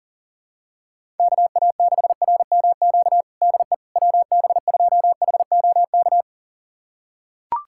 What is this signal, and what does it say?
Morse code, heard as a keyed tone, about 40 wpm, 700 Hz. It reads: KA6RMQ DE WB2HOK
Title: USA Callsign DE Callsign - 6 Characters - 40wpm